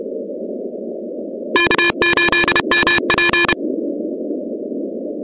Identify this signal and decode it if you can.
Morse code, heard as a keyed tone, about 30 wpm, 1750 Hz, with noise. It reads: K8MP
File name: contests / 2019-cw-wpx-kt5j-k5pi- -2330 0-17